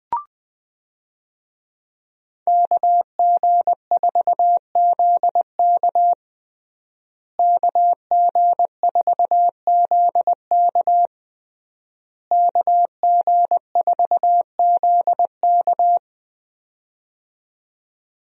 Morse code, heard as a keyed tone, about 20 wpm, 700 Hz. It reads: KG4ZK KG4ZK KG4ZK
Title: USA Call Signs - 5 Characters - Repeated 3x - 20wpm